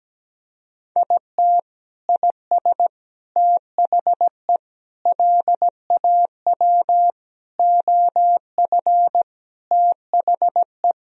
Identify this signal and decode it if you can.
Morse code, heard as a keyed tone, about 17 wpm, 700 Hz. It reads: IT IS THE LAW OF THE
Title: Sentences from Top 1000 Words - Review Part 3 - 17wpm